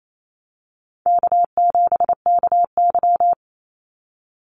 Morse code, heard as a keyed tone, about 28 wpm, 700 Hz. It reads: K7KY
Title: Call Signs - 28wpm